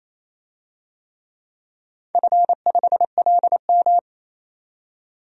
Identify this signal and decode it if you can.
Morse code, heard as a keyed tone, about 28 wpm, 700 Hz. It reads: F5LM
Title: DXCC-100 Call Signs - 4 Characters - 28wpm